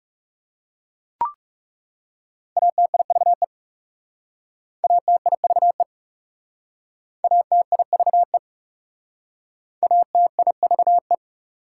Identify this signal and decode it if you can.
Morse code, heard as a keyed tone, about 40 wpm, 700 Hz. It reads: ATIVE ATIVE ATIVE ATIVE